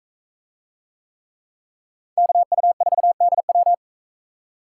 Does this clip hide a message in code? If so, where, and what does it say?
Yes: Morse code from 2.2 to 3.7 s, heard as a keyed tone, about 40 wpm, 700 Hz. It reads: KU4DW